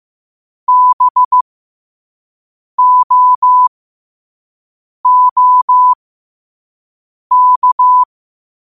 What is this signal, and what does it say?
Morse code, heard as a keyed tone, about 15 wpm, spaced as for 5 wpm, 1000 Hz. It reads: BOOK
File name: qotdfiles / 5wpm